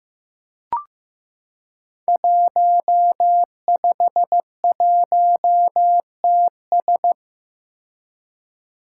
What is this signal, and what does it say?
Morse code, heard as a keyed tone, about 15 wpm, 700 Hz. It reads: 151TS
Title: Five Challenging Letter-Number Combinations 15wpm